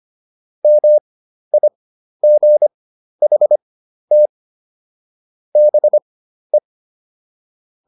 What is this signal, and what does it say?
Morse code, heard as a keyed tone, about 25 wpm, spaced as for 12 wpm, 600 Hz. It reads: MIGHT BE